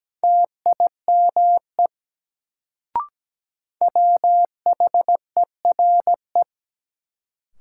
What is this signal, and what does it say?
Morse code, heard as a keyed tone, about 17 wpm, 700 Hz. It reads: TIME WHERE
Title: Sets of 2 Words - New Word Drill - 17wpm